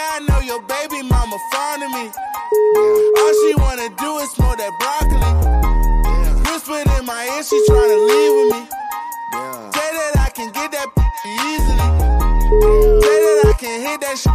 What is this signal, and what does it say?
Ringback tone (Europe, 425 Hz), with noise.